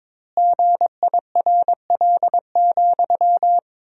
Morse code, heard as a keyed tone, about 22 wpm, 700 Hz. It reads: GIRL,
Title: 3x Word-Spacing - Sentences from Top 400 Words - 22wpm